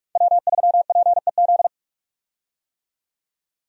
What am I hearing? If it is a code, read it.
Morse code, heard as a keyed tone, about 46 wpm, 700 Hz. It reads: W3PEZ
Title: USA Call Signs - 5 Characters - 45wpm